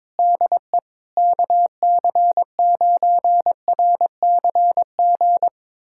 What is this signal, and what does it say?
Morse code, heard as a keyed tone, about 23 wpm, 700 Hz. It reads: DE KC9RCG
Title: USA Callsign DE Callsign - 6 Characters - 22wpm